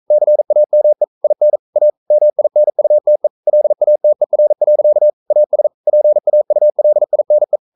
Morse code, heard as a keyed tone, about 42 wpm, 600 Hz. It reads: CAME IN A MINUTE LATER. AS PAULINE
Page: Romanen Never af Ken Follet på engelsk.